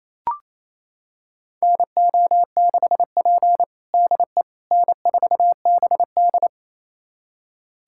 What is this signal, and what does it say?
Morse code, heard as a keyed tone, about 28 wpm, 690 Hz. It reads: NO6P DE N4BD
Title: USA Callsign DE Callsign - 4 Characters - 28wpm